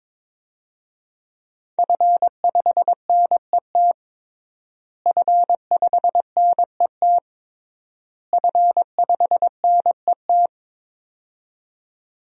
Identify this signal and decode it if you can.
Morse code, heard as a keyed tone, about 21 wpm, 700 Hz. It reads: F5NET F5NET F5NET
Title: DXCC-100 Call Signs - 5 Characters - Repeated 3x - 22wpm